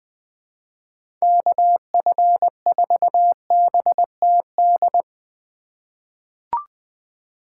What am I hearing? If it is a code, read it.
Morse code, heard as a keyed tone, about 21 wpm, 700 Hz. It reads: KF4BTD
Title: USA Call Signs - 6 Characters - 2x Character Spacing - 20wpm